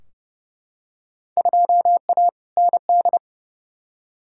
Morse code, heard as a keyed tone, about 30 wpm, 700 Hz. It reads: 2A ND
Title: Field Day Exchanges Encore - 30wpm